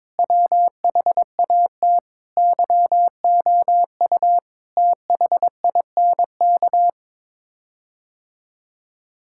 Morse code, heard as a keyed tone, about 22 wpm, 700 Hz. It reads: WHAT YOU THINK
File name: practice / 22wpm / Sets of 7 Words - 22wpm